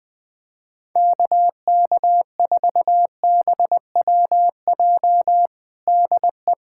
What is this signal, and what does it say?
Morse code, heard as a keyed tone, about 20 wpm, 700 Hz. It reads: KK4BWJ DE
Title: USA Callsign DE Callsign - 6 Characters - 20wpm